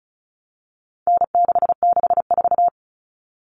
Morse code, heard as a keyed tone, about 34 wpm, 700 Hz. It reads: N664
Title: All Cut Numbers - 4-Digit 35wpm